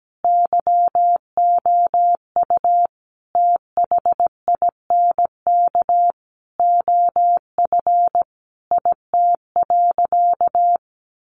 Morse code, heard as a keyed tone, about 17 wpm, 700 Hz. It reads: YOU THINK OF IT.
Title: Sentences from Top 200 Words 17wpm